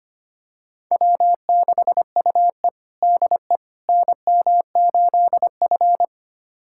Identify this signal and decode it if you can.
Morse code, heard as a keyed tone, about 25 wpm, 700 Hz. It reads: W6UE DE NM8F
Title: USA Callsign DE Callsign - 4 Characters - 25wpm